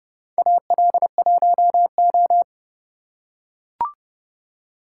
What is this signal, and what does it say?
Morse code, heard as a keyed tone, about 28 wpm, 700 Hz. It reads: AL1O
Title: USA Call Signs - 4 Characters - 3x Character Spacing - 30wpm